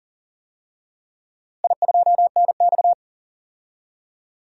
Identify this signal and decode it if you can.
Morse code, heard as a keyed tone, about 37 wpm, 700 Hz. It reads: I2NX
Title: DXCC-100 Call Signs - 4 Characters - Repeated 2x - 40wpm